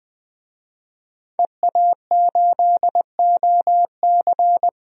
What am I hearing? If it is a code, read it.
Morse code, heard as a keyed tone, about 20 wpm, 700 Hz. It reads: EA8OC
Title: DXCC-100 Call Signs - 5 Characters - Repeated 3x - 20wpm